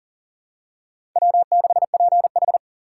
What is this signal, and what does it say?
Morse code, heard as a keyed tone, about 40 wpm, 700 Hz. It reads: W6PH
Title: USA Call Signs - 4 Characters - 6x Character Spacing - 40wpm